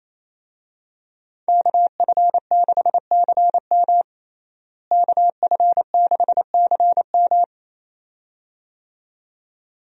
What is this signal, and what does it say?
Morse code, heard as a keyed tone, about 28 wpm, 700 Hz. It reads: KF6CM KF6CM
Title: USA Call Signs - 5 Characters - Repeated 2x - 28wpm